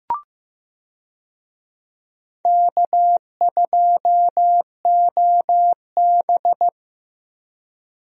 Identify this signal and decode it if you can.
Morse code, heard as a keyed tone, about 15 wpm, 700 Hz. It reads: K2OB
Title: USA Call Signs - 4 Characters - Repeated 2x - 15wpm